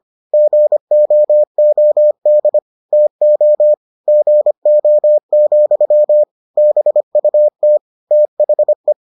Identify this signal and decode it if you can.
Morse code, heard as a keyed tone, about 25 wpm, 600 Hz. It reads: GOOD TO GO, BUT THE